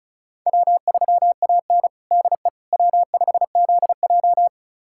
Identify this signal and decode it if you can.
Morse code, heard as a keyed tone, about 34 wpm, 700 Hz. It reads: W3AN DE W5ZJ
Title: USA Callsign DE Callsign - 4 Characters - 35wpm